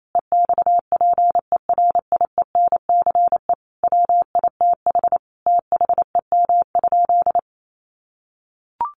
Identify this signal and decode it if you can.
Morse code, heard as a keyed tone, about 28 wpm, 700 Hz. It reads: EXPERIENCE WITH THEM?
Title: Sentences from Top 600 Words 28wpm